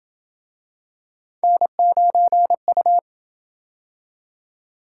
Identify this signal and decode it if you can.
Morse code, heard as a keyed tone, about 27 wpm, 700 Hz. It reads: N9U